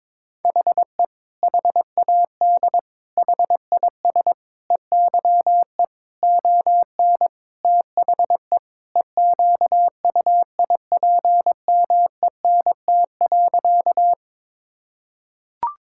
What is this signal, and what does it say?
Morse code, heard as a keyed tone, about 22 wpm, 700 Hz. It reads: HE HAD HIS EYE ON THE EQUIPMENT.
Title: Sentences from Top 1000 Words - Review Part 5 - 22wpm